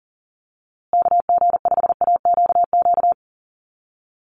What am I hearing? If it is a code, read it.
Morse code, heard as a keyed tone, about 40 wpm, 700 Hz. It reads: KG5AQQ
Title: USA Call Signs Repeated 3x 40wpm